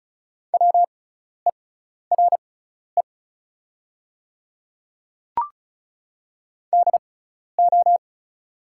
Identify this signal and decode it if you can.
Morse code, heard as a keyed tone, about 36 wpm, spaced as for 12 wpm, 690 Hz. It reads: WERE DO
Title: 6x Character Spacing - Top 100 Words - 35wpm